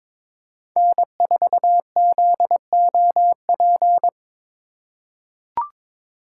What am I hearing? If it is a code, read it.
Morse code, heard as a keyed tone, about 22 wpm, 700 Hz. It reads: N4ZOP
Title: USA Call Signs - 5 Characters - 4x Character Spacing - 22wpm